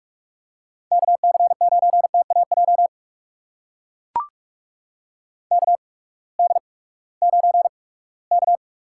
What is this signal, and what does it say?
Morse code, heard as a keyed tone, about 44 wpm, spaced as for 22 wpm, 700 Hz. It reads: KC9TAJ KD9K
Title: USA Call Signs - 6 Characters - 8x Character Spacing - 45wpm